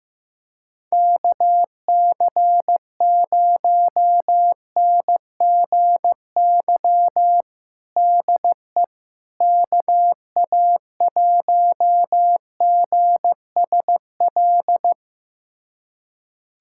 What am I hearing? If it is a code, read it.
Morse code, heard as a keyed tone, about 15 wpm, 700 Hz. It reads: KC0NGY DE KA1GSL
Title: USA Callsign DE Callsign - 6 Characters - 15wpm